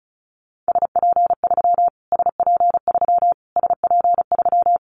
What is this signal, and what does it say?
Morse code, heard as a keyed tone, about 34 wpm, 700 Hz. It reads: SP3 SP3 SP3
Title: Callsign Prefixes for Top 25 DXCC - Repeated 3x - 35wpm